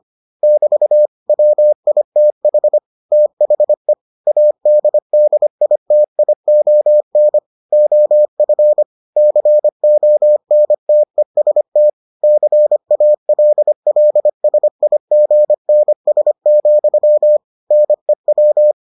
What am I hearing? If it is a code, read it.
Morse code, heard as a keyed tone, about 25 wpm, 600 Hz. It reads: <BT> WITH THE ADDITION OF CONTEST CALLSIGNS, NEW